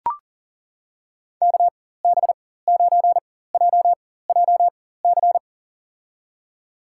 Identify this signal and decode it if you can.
Morse code, heard as a keyed tone, about 40 wpm, spaced as for 19 wpm, 700 Hz. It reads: KB9JJC
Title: USA Call Signs - 6 Characters - 4x Character Spacing - 40wpm